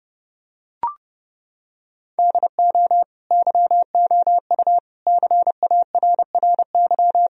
Morse code, heard as a keyed tone, about 32 wpm, 700 Hz. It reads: DO YOU CARRY
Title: Sentences from Top 1000 Words - Review Part 1 - 30wpm